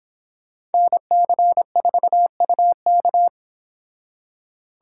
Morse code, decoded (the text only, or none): NC4UK